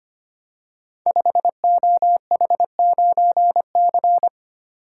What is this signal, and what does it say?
Morse code, heard as a keyed tone, about 25 wpm, 700 Hz. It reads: 5OH9C
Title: Code Groups - 5 Letter:Numbers - 25wpm